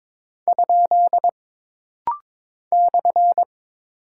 Morse code, heard as a keyed tone, about 22 wpm, 700 Hz. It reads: ? /
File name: practice / 22wpm / Single Punctuation-Prosign - Rapid-Fire - 22wpm